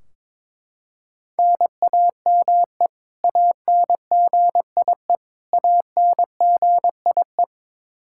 Morse code, heard as a keyed tone, about 21 wpm, 700 Hz. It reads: NAME ANGIE ANGIE
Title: K1USN SST QSO Practice 22wpm